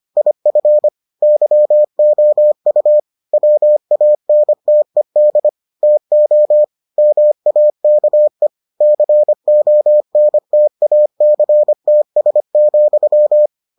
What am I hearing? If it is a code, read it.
Morse code, heard as a keyed tone, about 25 wpm, 600 Hz. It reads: IF YOU WANTED TO MAKE CONTACTS,